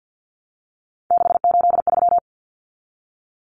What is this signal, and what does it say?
Morse code, heard as a keyed tone, about 50 wpm, 700 Hz. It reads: 683
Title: Common Cut Numbers - 3-Digit 50wpm